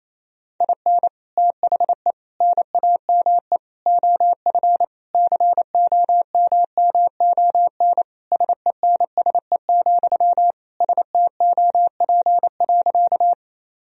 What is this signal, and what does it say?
Morse code, heard as a keyed tone, about 28 wpm, 700 Hz. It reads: IN THE NAME OF COMMON SENSE, STOP.
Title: Sentences from Top 1000 Words - Review Part 4 - 28wpm